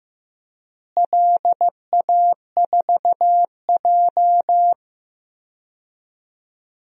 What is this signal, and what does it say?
Morse code, heard as a keyed tone, about 15 wpm, 700 Hz. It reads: LA4J
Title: DXCC-100 Call Signs - 4 Characters - Repeated 2x - 15wpm